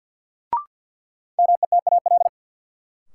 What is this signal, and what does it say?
Morse code, heard as a keyed tone, about 53 wpm, 700 Hz. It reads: METAL
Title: Sentences from Top 1600 Words - New Word Drill Alt - 50wpm